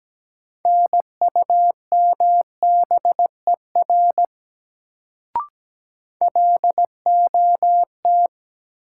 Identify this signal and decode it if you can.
Morse code, heard as a keyed tone, about 17 wpm, 700 Hz. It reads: NUMBER LOT
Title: Sets of 2 Words - New Word Drill - 17wpm